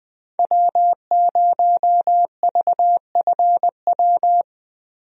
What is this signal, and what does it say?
Morse code, heard as a keyed tone, about 21 wpm, 700 Hz. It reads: W0VFW
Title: USA Call Signs - 5 Characters - Repeated 2x - 20wpm